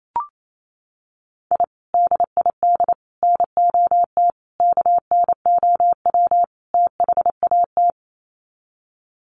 Morse code, heard as a keyed tone, about 28 wpm, 700 Hz. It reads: I DID NOT KNOW THAT
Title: Sets of 5 Words 28wpm